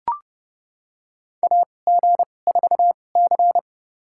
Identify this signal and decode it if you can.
Morse code, heard as a keyed tone, about 29 wpm, spaced as for 21 wpm, 700 Hz. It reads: AG4C
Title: USA Call Signs - 4 Characters - 2x Character Spacing - 30wpm